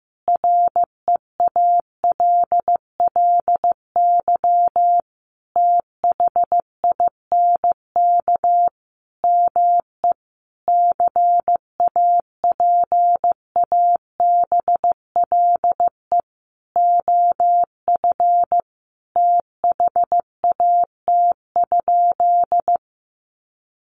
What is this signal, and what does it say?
Morse code, heard as a keyed tone, about 15 wpm, 700 Hz. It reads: REALLY THINK ME CAPABLE OF THAT?
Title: Sentences from Top 1200 Words 15wpm